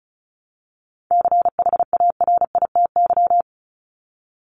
Morse code, heard as a keyed tone, about 34 wpm, 700 Hz. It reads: CHARITY
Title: Top 1501-1750 Words 35wpm